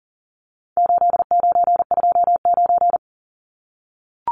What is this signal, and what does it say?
Morse code, heard as a keyed tone, about 40 wpm, 700 Hz. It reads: 8929
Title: Numbers - 4-Digit 40wpm